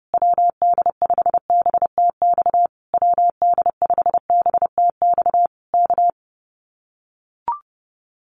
Morse code, heard as a keyed tone, about 30 wpm, 700 Hz. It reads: WD5BTX WD5BTX K
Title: Common QSO Phrases Encore 30wpm